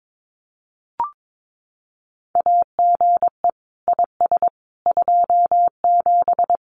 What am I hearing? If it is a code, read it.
Morse code, heard as a keyed tone, about 22 wpm, 700 Hz. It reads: AGE IS 27
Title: Common QSO Phrases Encore 22wpm